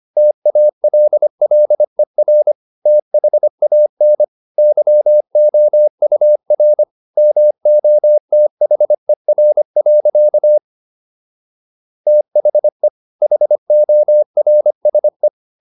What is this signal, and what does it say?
Morse code, heard as a keyed tone, about 25 wpm, 600 Hz. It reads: TALLER THAN YOUR MOTHER. THE HORSE